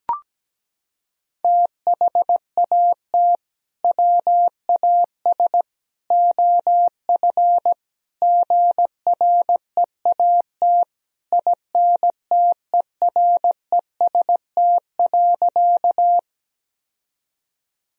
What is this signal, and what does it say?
Morse code, heard as a keyed tone, about 17 wpm, 700 Hz. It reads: THAT WAS OF GREAT INTEREST.
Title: Sentences from Top 1000 Words - Review Part 2 - 17wpm